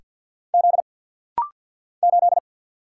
Morse code, heard as a keyed tone, about 47 wpm, 690 Hz. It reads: Z 8
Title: Single Letter-Number Alt - Rapid-Fire 50wpm